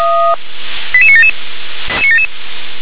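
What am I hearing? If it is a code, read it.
Five-tone selective call (ZVEI), heard as quick runs of five tones, tones of about 70 ms. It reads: 88977
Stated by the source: ZVEI five-tone selective call received on an SDR, channel 357.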